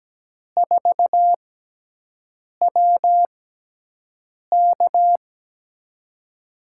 Morse code, heard as a keyed tone, about 17 wpm, spaced as for 6 wpm, 700 Hz. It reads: 4WK
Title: USA Call Signs - 5 Characters - 6x Character Spacing - 17wpm